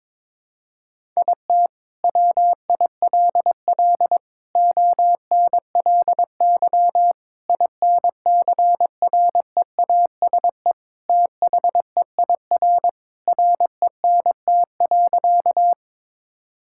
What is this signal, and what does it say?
Morse code, heard as a keyed tone, about 22 wpm, 700 Hz. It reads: IT WILL ONLY INCREASE THEIR RENT.
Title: Sentences from Top 1000 Words - Review Part 5 - 22wpm